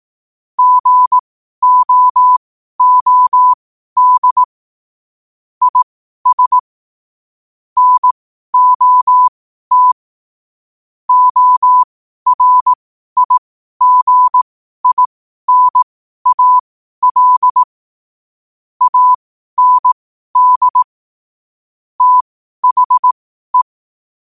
Morse code, heard as a keyed tone, about 18 wpm, spaced as for 12 wpm, 1000 Hz. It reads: GOOD IS NOT ORIGINAL AND THE
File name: qotdfiles / 12wpm